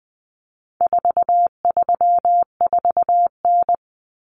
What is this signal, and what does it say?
Morse code, heard as a keyed tone, about 20 wpm, 700 Hz. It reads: 434N
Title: Common Cut Numbers - 4-Digit 20wpm